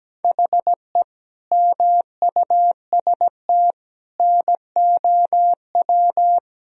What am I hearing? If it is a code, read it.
Morse code, heard as a keyed tone, about 17 wpm, 700 Hz. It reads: HE MUST NOW
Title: Sentences from Top 1000 Words - Review Part 1 - 17wpm